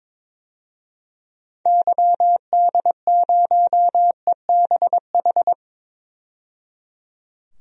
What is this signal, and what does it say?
Morse code, heard as a keyed tone, about 22 wpm, 700 Hz. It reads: YD0EBH
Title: DXCC-100 Call Signs - 6 Characters - 22wpm